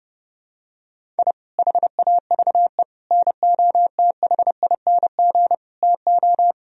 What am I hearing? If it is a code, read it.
Morse code, heard as a keyed tone, about 31 wpm, 700 Hz. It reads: I HAVE NOTHING TO
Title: Sentences from Top 1000 Words - Review Part 1 - 30wpm